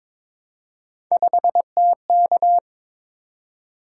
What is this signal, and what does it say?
Morse code, heard as a keyed tone, about 22 wpm, 700 Hz. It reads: 5TK
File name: practice / 22wpm / Three Letter-Number Combinations - 22wpm